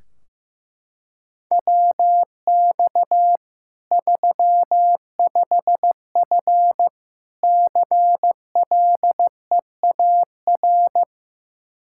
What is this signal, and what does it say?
Morse code, heard as a keyed tone, about 15 wpm, 700 Hz. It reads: WX 35F CLEAR
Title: K1USN SST QSO Practice 15wpm